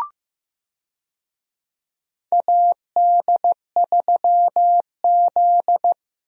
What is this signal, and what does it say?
Morse code, heard as a keyed tone, about 15 wpm, 700 Hz. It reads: AD3Z
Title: USA Call Signs - 4 Characters - Repeated 2x - 15wpm